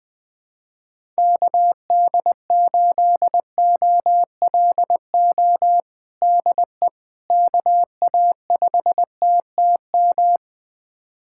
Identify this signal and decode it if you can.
Morse code, heard as a keyed tone, about 20 wpm, 700 Hz. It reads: KD8OLO DE KA5TTM
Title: USA Callsign DE Callsign 20wpm